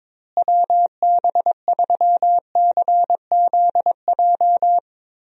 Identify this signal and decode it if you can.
Morse code, heard as a keyed tone, about 22 wpm, 700 Hz. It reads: WB3CZJ
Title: USA Call Signs - 6 Characters - Repeated 2x - 22wpm